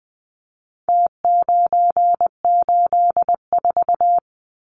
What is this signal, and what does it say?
Morse code, heard as a keyed tone, about 20 wpm, 700 Hz. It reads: T984
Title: Numbers - Contest Style - 4-Digit 20wpm